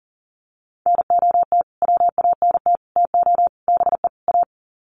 Morse code, heard as a keyed tone, about 37 wpm, 700 Hz. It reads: NOT WANT TO BE A
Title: Sets of 5 Words Encore 40wpm